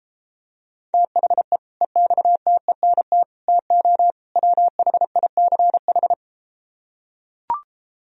Morse code, heard as a keyed tone, about 33 wpm, 700 Hz. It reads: THE EXTENT TO WHICH